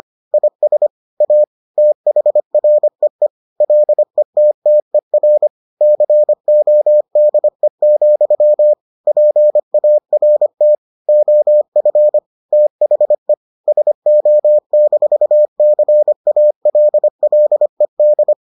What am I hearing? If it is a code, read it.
Morse code, heard as a keyed tone, about 25 wpm, 600 Hz. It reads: IS A THREE LETTER CODE, PART OF THE SO-CALLED